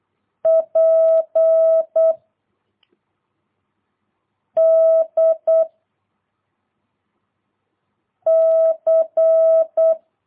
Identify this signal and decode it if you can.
Morse code, heard as a keyed tone, about 8 wpm, 660 Hz. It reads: P D C
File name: jh8cbh / 2022cw